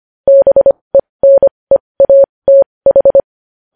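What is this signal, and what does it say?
Morse code, heard as a keyed tone, about 25 wpm, spaced as for 20 wpm, 550 Hz. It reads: BENEATH
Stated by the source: Morse code news headlines practice at 20.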